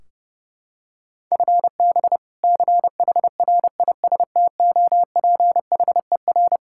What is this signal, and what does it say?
Morse code, heard as a keyed tone, about 32 wpm, 700 Hz. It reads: FB CHRISTOPHER
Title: K1USN SST QSO Practice 30wpm